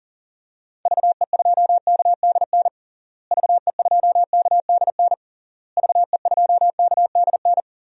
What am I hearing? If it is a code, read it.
Morse code, heard as a keyed tone, about 40 wpm, 700 Hz. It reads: VE2KDN VE2KDN VE2KDN
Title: DXCC-100 Call Signs - 6 Characters - Repeated 3x - 40wpm